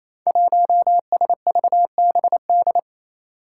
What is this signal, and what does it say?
Morse code, heard as a keyed tone, about 28 wpm, 700 Hz. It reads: 1SVBD
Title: Five Challenging Letter-Number Combinations 28wpm